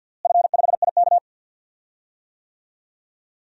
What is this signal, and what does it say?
Morse code, heard as a keyed tone, about 53 wpm, 700 Hz. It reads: F5IK
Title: DXCC-100 Call Signs - 4 Characters - Repeated 2x - 50wpm